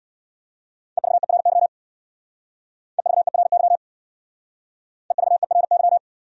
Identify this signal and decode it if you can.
Morse code, heard as a keyed tone, about 64 wpm, 700 Hz. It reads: EVERY EVERY EVERY